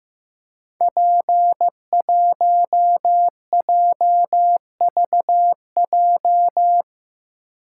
Morse code, heard as a keyed tone, about 15 wpm, 700 Hz. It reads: P1JVJ
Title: Code Groups - 5 Letter:Numbers - 15wpm